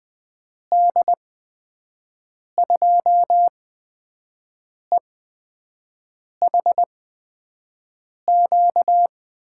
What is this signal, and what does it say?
Morse code, heard as a keyed tone, about 20 wpm, spaced as for 5 wpm, 700 Hz. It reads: D2EHQ
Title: USA Call Signs - 6 Characters - 8x Character Spacing - 20wpm